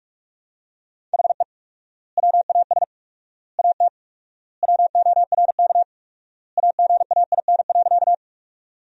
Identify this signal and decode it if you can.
Morse code, heard as a keyed tone, about 46 wpm, 700 Hz. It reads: HE WAS AT WORK AGAIN.
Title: 4x Word-Spacing - Sentences from Top 200 Words - 45wpm